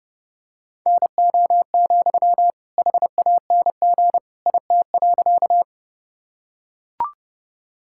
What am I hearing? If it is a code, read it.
Morse code, heard as a keyed tone, about 30 wpm, 700 Hz. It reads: NO, HANG IT.
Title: Sentences from Top 1600 Words 30wpm